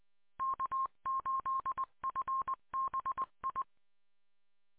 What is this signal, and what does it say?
Morse code, heard as a keyed tone, about 20 wpm, 1070 Hz. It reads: K8FBI